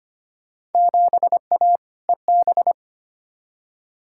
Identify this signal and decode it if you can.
Morse code, heard as a keyed tone, about 25 wpm, 700 Hz. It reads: 7A EB
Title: Field Day Exchanges Encore - 25wpm